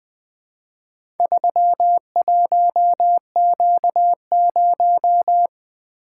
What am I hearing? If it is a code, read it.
Morse code, heard as a keyed tone, about 20 wpm, 700 Hz. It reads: 31Q0
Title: Four Letter-Number Combinations - 20wpm